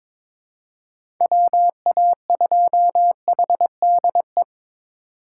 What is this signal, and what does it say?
Morse code, heard as a keyed tone, about 21 wpm, 700 Hz. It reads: WA2HDE
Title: USA Call Signs - 6 Characters - 4x Character Spacing - 22wpm